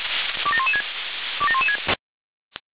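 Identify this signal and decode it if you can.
Five-tone selective call (ZVEI), heard as quick runs of five tones, tones of about 70 ms. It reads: 27116 27116
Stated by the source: ZVEI five-tone selective call 27002 received on SDR.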